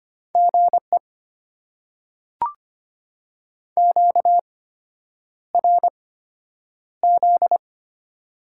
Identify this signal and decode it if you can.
Morse code, heard as a keyed tone, about 25 wpm, spaced as for 9 wpm, 700 Hz. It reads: GE QRZ
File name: Top 90 QSO Elements - 8x Character Spacing - 25wpm